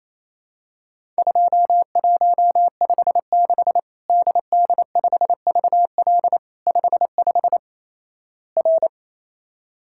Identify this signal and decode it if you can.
Morse code, heard as a keyed tone, about 28 wpm, 700 Hz. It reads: 2156 DD5VL 55 R
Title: Worked All Europe DX Contest - QTC Exchange - 28wpm